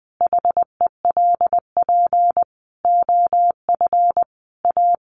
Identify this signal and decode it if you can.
Morse code, heard as a keyed tone, about 20 wpm, 700 Hz. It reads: HELP OF A
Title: Sets of 5 Words 20wpm